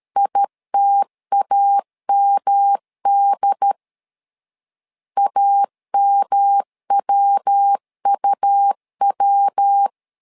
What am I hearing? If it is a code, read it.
Morse code, heard as a keyed tone, about 13 wpm, 800 Hz. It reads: ITAMD AMWUW
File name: LEZIONI / 3 MWRO / 3 MW 12wpm 1633